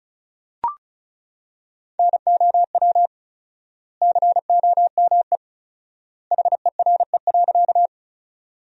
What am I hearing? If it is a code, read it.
Morse code, heard as a keyed tone, about 36 wpm, 700 Hz. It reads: NOW COME HERE.
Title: 4x Word-Spacing - Sentences from Top 200 Words - 35wpm